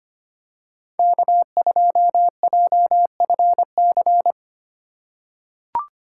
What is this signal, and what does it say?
Morse code, heard as a keyed tone, about 25 wpm, 700 Hz. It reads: K2JFC
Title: USA Call Signs - 3x Character Spacing - 25wpm